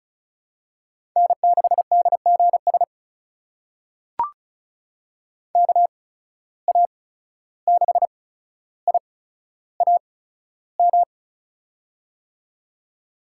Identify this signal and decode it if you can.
Morse code, heard as a keyed tone, about 36 wpm, spaced as for 14 wpm, 700 Hz. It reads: N6DGS KA6IAM
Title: USA Call Signs - 8x Character Spacing - 35wpm